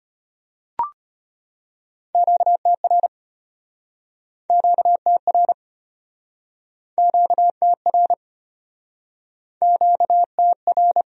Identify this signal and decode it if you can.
Morse code, heard as a keyed tone, about 36 wpm, 700 Hz. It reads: QTR QTR QTR QTR